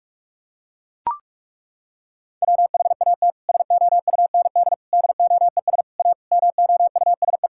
Morse code, heard as a keyed tone, about 41 wpm, 700 Hz. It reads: WHAT SOUND DOES A MOUSE
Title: Sentences from Top 1000 Words - Review Part 5 - 45wpm